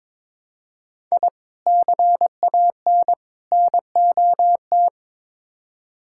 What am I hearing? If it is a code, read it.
Morse code, heard as a keyed tone, about 22 wpm, 700 Hz. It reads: I CAN NOT
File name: practice / 22wpm / Build A Phrase - 5-Word Phrases - 22wpm